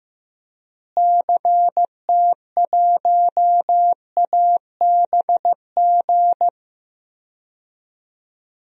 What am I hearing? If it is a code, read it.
Morse code, heard as a keyed tone, about 15 wpm, 700 Hz. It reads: CT1ABG
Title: DXCC-100 Call Signs - 6 Characters - Repeated 2x - 15wpm